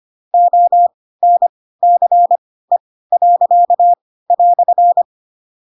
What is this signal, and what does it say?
Morse code, heard as a keyed tone, about 25 wpm, spaced as for 16 wpm, 700 Hz. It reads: ONCE."
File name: Pinocchio / Pinocchio ch4 10 25 ik1hge